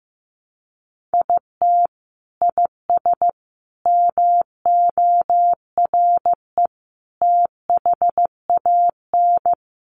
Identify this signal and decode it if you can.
Morse code, heard as a keyed tone, about 15 wpm, 700 Hz. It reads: IT IS MORE THAN
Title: Sentences from Top 1400 Words 15wpm